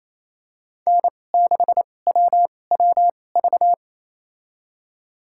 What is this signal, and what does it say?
Morse code, heard as a keyed tone, about 28 wpm, spaced as for 20 wpm, 700 Hz. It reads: N6WWV
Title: USA Call Signs - 5 Characters - 2x Character Spacing - 28wpm